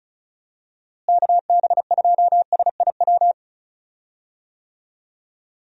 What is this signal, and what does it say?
Morse code, heard as a keyed tone, about 36 wpm, 700 Hz. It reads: KB2SIW
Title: USA Call Signs - 6 Characters - Repeated 3x - 35wpm